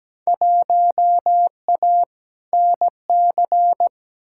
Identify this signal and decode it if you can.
Morse code, heard as a keyed tone, about 17 wpm, 700 Hz. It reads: 1A NC
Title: Field Day Exchanges Encore - 17wpm